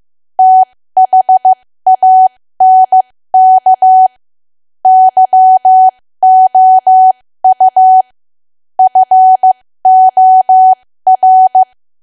Morse code, heard as a keyed tone, about 15 wpm, spaced as for 13 wpm, 750 Hz. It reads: THANK YOU FOR
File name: cwa / W1AW-Level2 / 13WPM